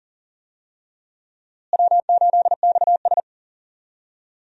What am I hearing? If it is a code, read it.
Morse code, heard as a keyed tone, about 40 wpm, 700 Hz. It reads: W8XS
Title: USA Call Signs - 4 Characters - Repeated 3x - 40wpm